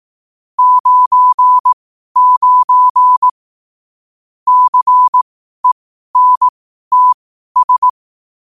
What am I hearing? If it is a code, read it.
Morse code, heard as a keyed tone, about 18 wpm, spaced as for 12 wpm, 1000 Hz. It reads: 99 CENTS
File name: qotdfiles / 12wpm